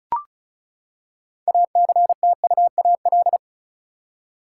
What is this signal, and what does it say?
Morse code, heard as a keyed tone, about 34 wpm, 700 Hz. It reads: ACTUAL
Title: Six Letter Words - 35wpm